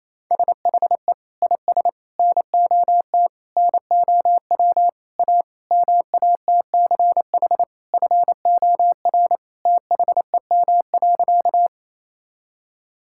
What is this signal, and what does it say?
Morse code, heard as a keyed tone, about 27 wpm, 700 Hz. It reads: SHE IS NOT NOW A MATCH FOR THEM.
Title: Sentences from Top 1600 Words 28wpm